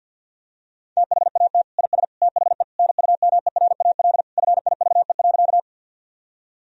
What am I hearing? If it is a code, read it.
Morse code, heard as a keyed tone, about 50 wpm, 700 Hz. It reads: THAT IS THE NUMERAL FIVE.
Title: Sentences from Top 1000 Words - Review Part 2 - 50wpm